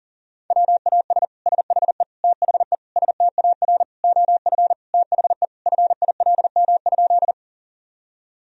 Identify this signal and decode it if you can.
Morse code, heard as a keyed tone, about 40 wpm, 700 Hz. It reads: WAS SHE THE STAR OF THE FILM?
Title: Sentences from Top 1000 Words - Review Part 3 - 40wpm